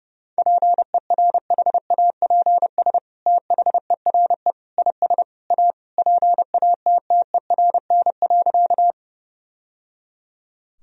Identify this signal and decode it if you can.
Morse code, heard as a keyed tone, about 32 wpm, 700 Hz. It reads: PERHAPS THERE IS A PATTERN.
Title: Sentences from Top 1000 Words - Review Part 2 - 30wpm